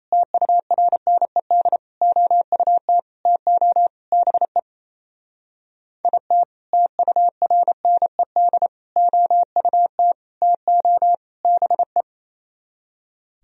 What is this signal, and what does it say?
Morse code, heard as a keyed tone, about 36 wpm, 700 Hz. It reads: TURNED OUT TO BE IT TURNED OUT TO BE